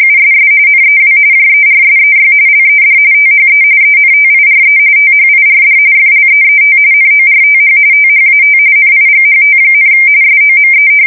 RTTY, about 45.45 baud. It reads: USS NEW JERSEY LED THE ENTIRE THIRD FLEET NORTH THAT NIGHT, SOME
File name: fsk170 / battleshipnj